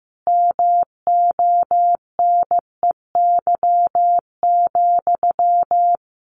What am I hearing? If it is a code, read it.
Morse code, heard as a keyed tone, about 15 wpm, 700 Hz. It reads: MONEY,
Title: Sentences from Top 900 Words 15wpm